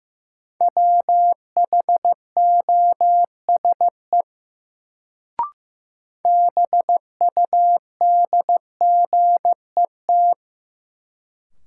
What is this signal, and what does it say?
Morse code, heard as a keyed tone, about 15 wpm, 700 Hz. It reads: WHOSE BUDGET
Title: Sentences from Top 1200 Words - New Word Drill Alt - 15wpm